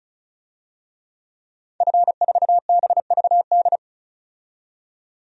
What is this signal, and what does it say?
Morse code, heard as a keyed tone, about 34 wpm, 700 Hz. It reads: F4BVD
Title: DXCC-100 Call Signs - 5 Characters - 35wpm